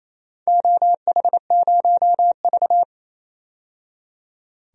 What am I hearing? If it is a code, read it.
Morse code, heard as a keyed tone, about 28 wpm, 700 Hz. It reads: OH0V
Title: DXCC-100 Call Signs - 4 Characters - 28wpm